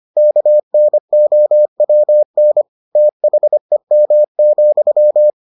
Morse code, heard as a keyed tone, about 25 wpm, 600 Hz. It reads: KNOWN THEM,